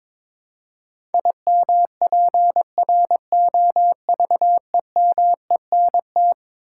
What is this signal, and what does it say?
Morse code, heard as a keyed tone, about 22 wpm, 700 Hz. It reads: IMPROVEMENT
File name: practice / 22wpm / Eleven Letter Words - 22wpm